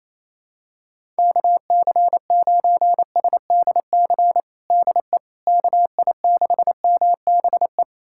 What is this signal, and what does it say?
Morse code, heard as a keyed tone, about 27 wpm, 700 Hz. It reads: KC9SDC DE KI6MBE
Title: USA Callsign DE Callsign - 6 Characters - 28wpm